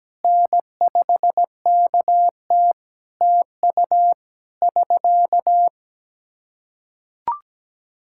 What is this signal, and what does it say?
Morse code, heard as a keyed tone, about 17 wpm, 700 Hz. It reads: N5KT TU <SK>
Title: Practice QSO - Exchange 4 - Ending - 17wpm